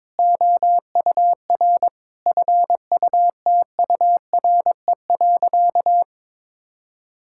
Morse code, heard as a keyed tone, about 22 wpm, 700 Hz. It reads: OUR FUTURE.
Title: Sentences from Top 1000 Words - Review Part 6 - 22wpm